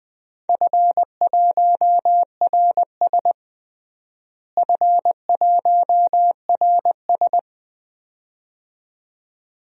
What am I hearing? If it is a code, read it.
Morse code, heard as a keyed tone, about 20 wpm, 700 Hz. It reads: F1RS F1RS
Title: DXCC-100 Call Signs - 4 Characters - Repeated 2x - 20wpm